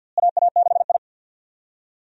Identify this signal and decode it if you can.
Morse code, heard as a keyed tone, about 50 wpm, 700 Hz. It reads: AA6I
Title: USA Call Signs - 4 Characters - Repeated 3x - 50wpm